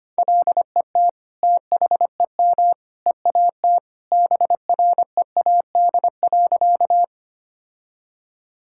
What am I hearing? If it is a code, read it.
Morse code, heard as a keyed tone, about 25 wpm, 700 Hz. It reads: LET THEM EAT BREAD.
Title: Sentences from Top 1600 Words 25wpm